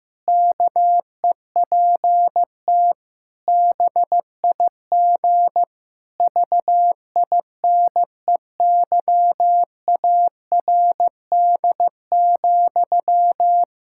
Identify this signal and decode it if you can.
Morse code, heard as a keyed tone, about 15 wpm, 700 Hz. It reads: KEPT BIG VINEYARD,
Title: Pangram Challenge v2 - 15wpm